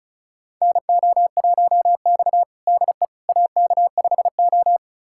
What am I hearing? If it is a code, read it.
Morse code, heard as a keyed tone, about 36 wpm, 700 Hz. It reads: NO1X DE AK5O
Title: USA Callsign DE Callsign - 4 Characters - 35wpm